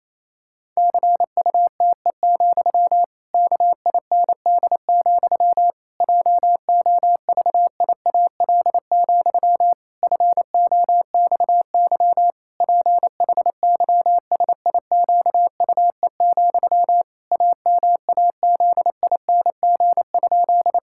Morse code, heard as a keyed tone, about 28 wpm, 700 Hz. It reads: CUTE, KIND, JOVIAL, FOXY PHYSIQUE, AMAZING?
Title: Pangram Challenge v2 - 28wpm